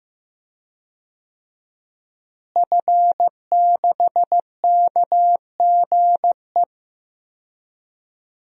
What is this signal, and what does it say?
Morse code, heard as a keyed tone, about 15 wpm, 700 Hz. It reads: F6KGE
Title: DXCC-100 Call Signs - 5 Characters - Repeated 2x - 15wpm